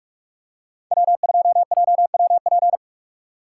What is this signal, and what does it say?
Morse code, heard as a keyed tone, about 43 wpm, 700 Hz. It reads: W2JWP
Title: USA Call Signs - 5 Characters - 6x Character Spacing - 45wpm